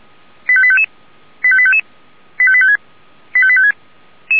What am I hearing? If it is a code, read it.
Five-tone selective call (ZVEI), heard as quick runs of five tones, tones of about 70 ms. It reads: 85855 85855 85856 85856